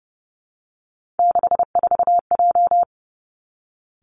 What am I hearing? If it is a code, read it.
Morse code, heard as a keyed tone, about 31 wpm, 700 Hz. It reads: 64J
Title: Three Challenging Letter-Number Combinations 30wpm